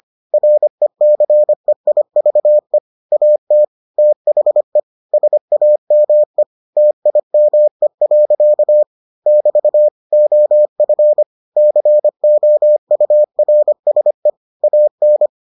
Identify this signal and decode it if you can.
Morse code, heard as a keyed tone, about 25 wpm, 600 Hz. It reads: RECEIVE AT THE SAME TIME. <BT> OF COURSE AN